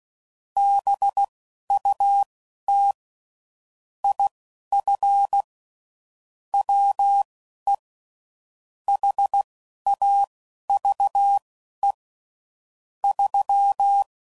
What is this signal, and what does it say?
Morse code, heard as a keyed tone, about 16 wpm, spaced as for 11 wpm, 800 Hz. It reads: BUT IF WE HAVE 3
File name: CW Code Groups @10WPM/Technical Subjects